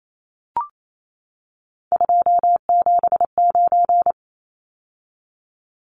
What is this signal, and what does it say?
Morse code, heard as a keyed tone, about 28 wpm, 700 Hz. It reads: 279
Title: Numbers - 3-Digit 28wpm